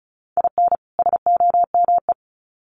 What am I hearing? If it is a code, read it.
Morse code, heard as a keyed tone, about 34 wpm, 700 Hz. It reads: IN SOME
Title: Sets of 2 Words 35wpm